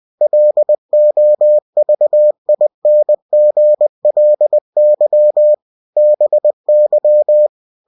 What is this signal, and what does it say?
Morse code, heard as a keyed tone, about 20 wpm, 600 Hz. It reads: LOVINGLY BY
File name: cw-books / winnie-the-pooh / winnie-20wpm-page-0015